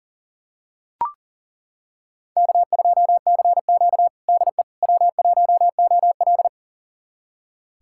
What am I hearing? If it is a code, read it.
Morse code, heard as a keyed tone, about 40 wpm, 700 Hz. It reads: K2CQ DE W1OL
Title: USA Callsign DE Callsign - 4 Characters - 40wpm